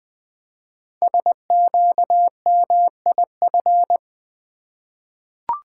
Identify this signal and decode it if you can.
Morse code, heard as a keyed tone, about 20 wpm, 700 Hz. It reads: SQMIF
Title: Code Groups - 5 Letters - 20wpm